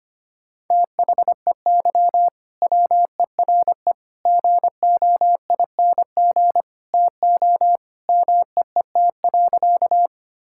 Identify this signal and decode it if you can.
Morse code, heard as a keyed tone, about 25 wpm, 700 Hz. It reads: THEY WERE GOING TO MEET.
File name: practice / 25wpm / Sentences with 3-Word Phrases - 25wpm